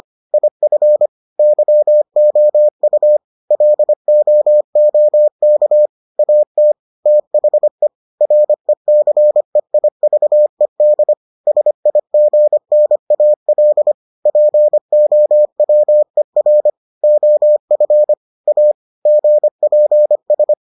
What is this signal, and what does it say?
Morse code, heard as a keyed tone, about 25 wpm, 600 Hz. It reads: IF YOU LOOK AT THE RECEIVED SIGNAL POWER OF A GPS